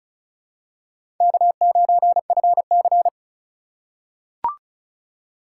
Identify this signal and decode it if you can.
Morse code, heard as a keyed tone, about 36 wpm, 700 Hz. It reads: K9FC
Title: USA Call Signs - 4 Characters - 8x Character Spacing - 35wpm